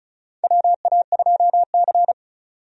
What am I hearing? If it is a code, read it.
Morse code, heard as a keyed tone, about 36 wpm, 700 Hz. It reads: WA2C
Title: USA Call Signs - 4 Characters - 35wpm